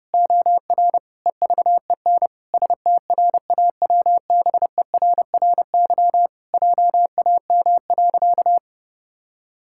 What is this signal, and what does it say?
Morse code, heard as a keyed tone, about 30 wpm, 700 Hz. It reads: OR EVEN STRAWBERRY JAM.
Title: Pangram Challenge v2 - 30wpm